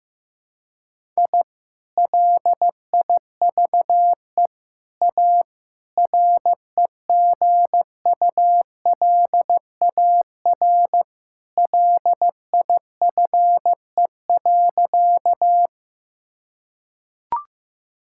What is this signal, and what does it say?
Morse code, heard as a keyed tone, about 15 wpm, 700 Hz. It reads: I LIVE A REGULAR LIFE.
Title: Sentences from Top 1000 Words - Review Part 6 - 15wpm